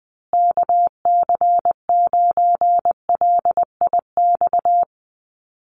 Morse code, heard as a keyed tone, about 20 wpm, 700 Hz. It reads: KC9LIX
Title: USA Call Signs Repeated 3x 20wpm